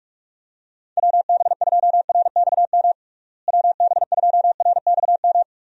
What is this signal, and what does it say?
Morse code, heard as a keyed tone, about 46 wpm, 700 Hz. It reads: WB2RXM WB2RXM
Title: USA Call Signs - 6 Characters - Repeated 2x - 45wpm